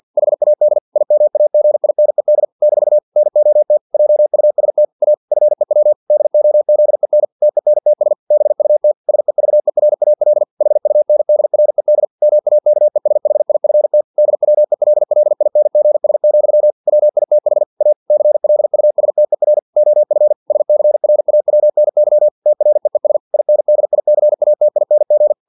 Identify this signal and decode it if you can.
Morse code, heard as a keyed tone, about 50 wpm, 600 Hz. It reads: HAD IMAGINED - NOT JUST A FEW DOZEN TENTS BUT SEVERAL HUNDRED MAKESHIFT DWELLINGS, WITH A CLUSTER OF SCRAWNY TREES INDICATING
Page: Never 49wpm